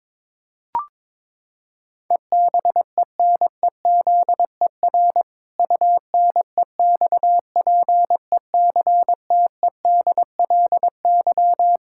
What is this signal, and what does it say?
Morse code, heard as a keyed tone, about 21 wpm, 700 Hz. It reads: EBENEZER UNEXPECTEDLY
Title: Pangram Challenge v2 - 22wpm